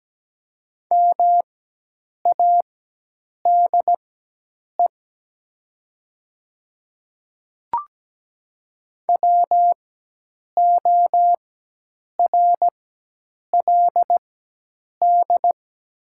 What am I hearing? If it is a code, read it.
Morse code, heard as a keyed tone, about 17 wpm, spaced as for 8 wpm, 700 Hz. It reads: MADE WORLD
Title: 4x Character Spacing - Top 200 Words - 17wpm